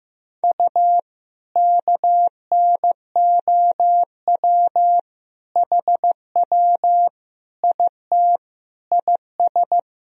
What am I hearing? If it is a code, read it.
Morse code, heard as a keyed tone, about 15 wpm, 700 Hz. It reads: U KNOW HW IT IS
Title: Sentences from Top 100 Words and Abbreviations 15wpm